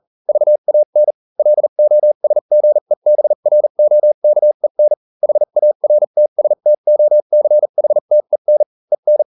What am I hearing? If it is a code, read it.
Morse code, heard as a keyed tone, about 40 wpm, spaced as for 35 wpm, 600 Hz. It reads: VAN LOSGEBROKEN HARTSTOCHTEN EN